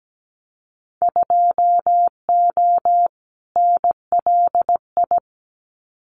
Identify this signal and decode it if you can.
Morse code, heard as a keyed tone, about 17 wpm, 700 Hz. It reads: 2O NLI
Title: Winter Field Day Exchanges 17wpm